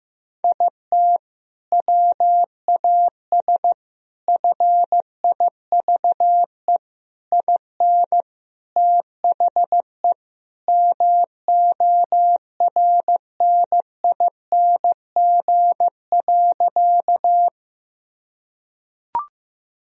Morse code, heard as a keyed tone, about 15 wpm, 700 Hz. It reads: IT WAS FIVE IN THE MORNING.
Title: Sentences from Top 1000 Words - Review Part 2 - 15wpm